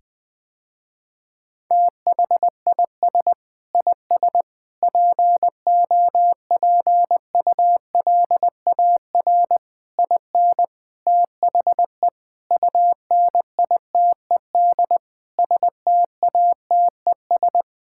Morse code, heard as a keyed tone, about 20 wpm, 700 Hz. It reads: THIS IS POPULAR IN THE UNITED STATES
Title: Sentences with 5-Word Phrases - Spoken Beginning - 20wpm